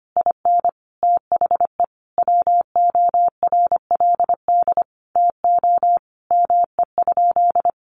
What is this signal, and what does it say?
Morse code, heard as a keyed tone, about 25 wpm, 700 Hz. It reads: IN THE WORLD TO ME?
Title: Sentences from Top 900 Words 25wpm